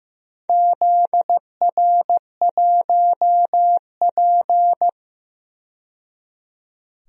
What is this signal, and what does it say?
Morse code, heard as a keyed tone, about 15 wpm, 700 Hz. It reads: ZR1P
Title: DXCC-100 Call Signs - 4 Characters - 15wpm